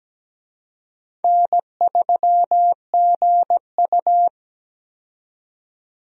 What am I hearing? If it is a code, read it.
Morse code, heard as a keyed tone, about 17 wpm, 700 Hz. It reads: N3GU
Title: USA Call Signs - 4 Characters - 17wpm